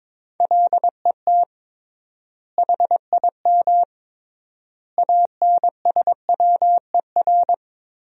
Morse code, heard as a keyed tone, about 22 wpm, 700 Hz. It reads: LET HIM ANSWER
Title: 3x Word-Spacing - Sentences from Top 300 Words - 22wpm